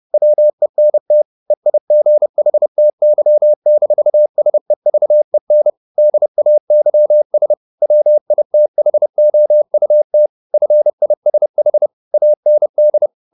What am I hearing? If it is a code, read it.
Morse code, heard as a keyed tone, about 30 wpm, 600 Hz. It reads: WENT EIGHTY-SEVEN DAYS WITHOUT FISH AND
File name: the old man and the sea / the old man and the sea 1 30wpm